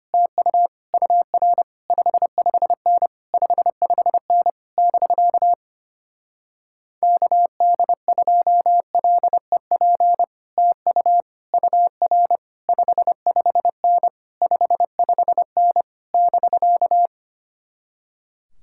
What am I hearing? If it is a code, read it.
Morse code, heard as a keyed tone, about 26 wpm, 700 Hz. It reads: TU UR 55N 55N <BK> KD2LEP TU UR 55N 55N <BK>
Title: Speed-Racing Common QSO Phrases Encore - 25wpm